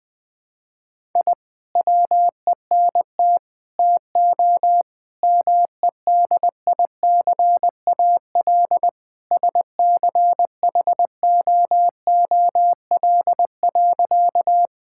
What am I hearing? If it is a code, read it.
Morse code, heard as a keyed tone, about 20 wpm, 700 Hz. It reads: I WENT TO MEDICAL SCHOOL.
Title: Sentences from Top 1000 Words - Review Part 5 - 20wpm